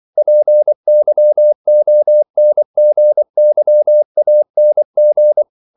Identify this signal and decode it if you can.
Morse code, heard as a keyed tone, about 24 wpm, 600 Hz. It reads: PYONGYANG
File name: CW / Practice / NationalCapitals24